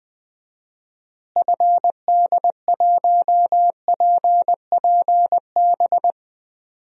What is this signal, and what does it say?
Morse code, heard as a keyed tone, about 20 wpm, 700 Hz. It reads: FD1PPB
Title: DXCC-100 Call Signs - 6 Characters - Repeated 3x - 20wpm